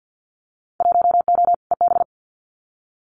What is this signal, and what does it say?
Morse code, heard as a keyed tone, about 48 wpm, 700 Hz. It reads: 1O EB
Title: Winter Field Day Exchanges 50wpm